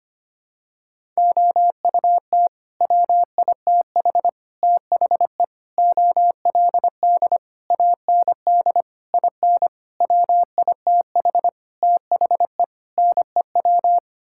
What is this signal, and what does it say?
Morse code, heard as a keyed tone, about 25 wpm, 700 Hz. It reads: OUT WITH THE OLD AND IN WITH THE NEW
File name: Sets of 9 Words - 25wpm